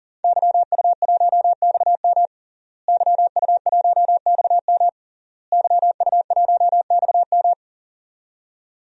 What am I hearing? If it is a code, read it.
Morse code, heard as a keyed tone, about 40 wpm, 700 Hz. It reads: YU1XM YU1XM YU1XM
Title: DXCC-100 Call Signs - 5 Characters - Repeated 3x - 40wpm